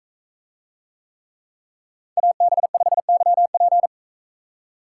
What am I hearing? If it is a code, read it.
Morse code, heard as a keyed tone, about 42 wpm, 700 Hz. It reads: AB5YP